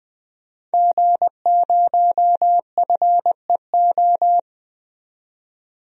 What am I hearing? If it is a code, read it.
Morse code, heard as a keyed tone, about 21 wpm, 700 Hz. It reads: G0FEO
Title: DXCC-100 Call Signs - 5 Characters - 20wpm